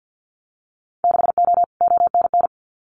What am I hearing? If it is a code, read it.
Morse code, heard as a keyed tone, about 50 wpm, 700 Hz. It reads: 6O ONN
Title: Winter Field Day Exchanges Encore 50wpm